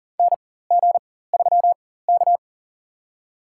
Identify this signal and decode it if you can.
Morse code, heard as a keyed tone, about 38 wpm, spaced as for 18 wpm, 700 Hz. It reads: NG3K
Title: USA Call Signs - 4 Characters - 4x Character Spacing - 40wpm